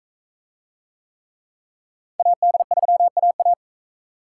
Morse code, heard as a keyed tone, about 40 wpm, 700 Hz. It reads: AD3AA